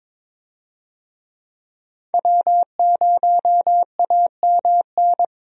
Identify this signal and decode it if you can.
Morse code, heard as a keyed tone, about 21 wpm, 700 Hz. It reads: W0AMN
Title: USA Call Signs - 5 Characters - 22wpm